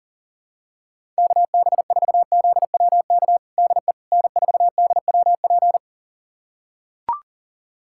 Morse code, heard as a keyed tone, about 40 wpm, 700 Hz. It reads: KB4ZWK DE N4DWP
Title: USA Callsign DE Callsign 40wpm